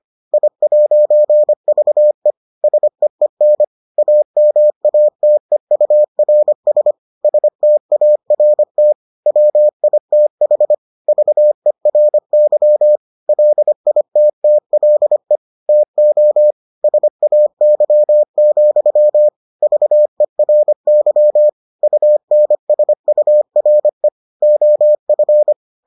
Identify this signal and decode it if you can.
Morse code, heard as a keyed tone, about 25 wpm, 600 Hz. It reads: I'VE SEEN AMATEURS START WITH VERY LITTLE TO SAY, VERY UNSURE OF